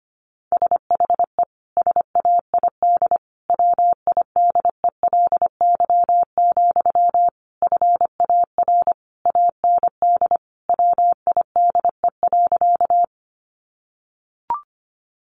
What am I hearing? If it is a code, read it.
Morse code, heard as a keyed tone, about 25 wpm, 700 Hz. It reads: SHE SAID WIDELY, FAR AND WIDE.
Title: Sentences from Top 1200 Words 25wpm